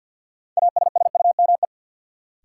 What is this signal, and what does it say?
Morse code, heard as a keyed tone, about 47 wpm, 700 Hz. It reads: ASSUME
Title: Sentences from Top 1500 Words - New Word Drill - 50wpm